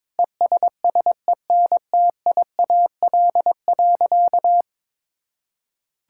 Morse code, heard as a keyed tone, about 22 wpm, 700 Hz. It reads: ESSENTIAL.
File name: practice / 22wpm / Sentences with 8-Word Phrases - 22wpm